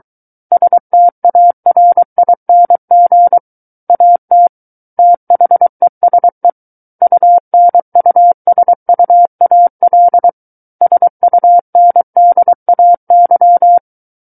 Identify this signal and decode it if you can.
Morse code, heard as a keyed tone, about 23 wpm, 700 Hz. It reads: STARING AT THESE UNUSUAL SUNDAY
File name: Book One / 14-Chapter 14 - 23 WPM